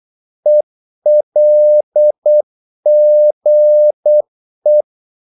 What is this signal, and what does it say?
Morse code, heard as a keyed tone, about 8 wpm, 600 Hz. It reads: ELGE
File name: Korte ord DK 8wpm